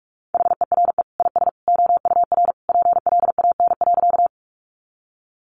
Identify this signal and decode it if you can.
Morse code, heard as a keyed tone, about 44 wpm, 700 Hz. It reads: HERE IS OUR PLAN.
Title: Sentences from Top 500 Words 45wpm